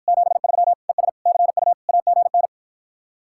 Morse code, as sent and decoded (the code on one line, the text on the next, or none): --... ...-- / . ... / -.-. ..- / .- --. -.
73 ES CU AGN